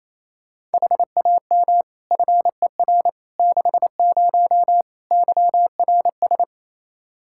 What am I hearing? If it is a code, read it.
Morse code, heard as a keyed tone, about 28 wpm, 700 Hz. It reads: HAM FER 60 YRS
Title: Practice QSO - Exchange 3 - More Info - 28wpm